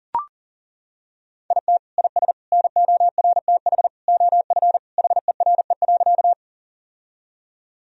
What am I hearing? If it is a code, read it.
Morse code, heard as a keyed tone, about 40 wpm, 700 Hz. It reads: IT IS NORTH OF HERE.
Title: Sentences from Top 1000 Words - Review Part 1 - 40wpm